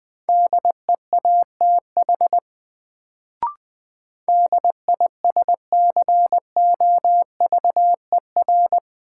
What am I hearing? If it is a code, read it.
Morse code, heard as a keyed tone, about 20 wpm, 700 Hz. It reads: DEATH DISCOVER
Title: Sentences from Top 1100 Words - New Word Drill Alt - 20wpm